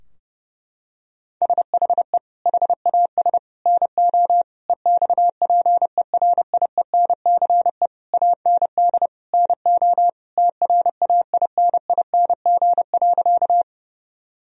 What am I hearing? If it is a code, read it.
Morse code, heard as a keyed tone, about 30 wpm, 700 Hz. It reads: SHE HAS NO EXPERIENCE AND NO TRAINING.
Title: Sentences from Top 1000 Words - Review Part 4 - 30wpm